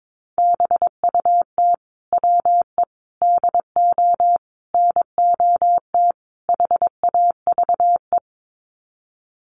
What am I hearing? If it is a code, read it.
Morse code, heard as a keyed tone, about 22 wpm, 700 Hz. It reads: BUT WE DO NOT HAVE
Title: Sets of 5 Words Encore2 22wpm